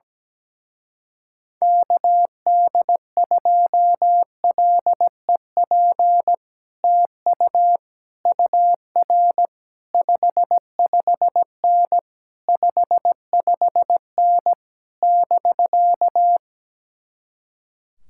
Morse code, heard as a keyed tone, about 17 wpm, 700 Hz. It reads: KD2LEP TU UR 55N 55N <BK>
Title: Speed-Racing Common QSO Phrases Encore - 17wpm